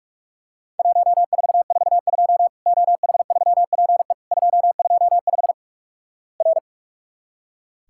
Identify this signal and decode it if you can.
Morse code, heard as a keyed tone, about 44 wpm, 700 Hz. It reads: 1442 OH3PE 225 R
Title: Worked All Europe DX Contest - QTC Exchange - 45wpm